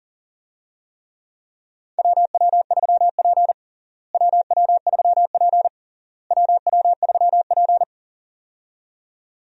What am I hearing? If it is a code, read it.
Morse code, heard as a keyed tone, about 40 wpm, 700 Hz. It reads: WW3P WW3P WW3P
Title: USA Call Signs - 4 Characters - Repeated 3x - 40wpm